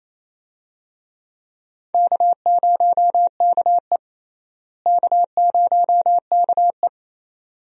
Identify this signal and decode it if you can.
Morse code, heard as a keyed tone, about 28 wpm, 700 Hz. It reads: K0KE K0KE
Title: USA Call Signs - 4 Characters - Repeated 2x - 28wpm